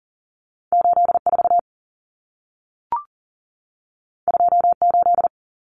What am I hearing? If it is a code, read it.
Morse code, heard as a keyed tone, about 40 wpm, 700 Hz. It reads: 84 28
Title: Numbers - 2-Digit 40wpm